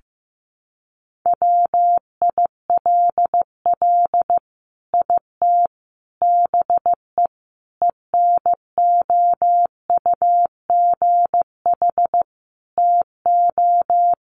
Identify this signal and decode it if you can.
Morse code, heard as a keyed tone, about 14 wpm, 700 Hz. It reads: WILL IT BE ENOUGH TO
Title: Sentences from Top 1100 Words 15wpm